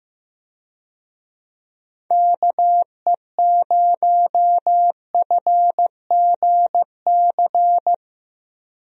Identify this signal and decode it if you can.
Morse code, heard as a keyed tone, about 16 wpm, 700 Hz. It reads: KE0FGC
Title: USA Call Signs - 6 Characters - Repeated 3x - 15wpm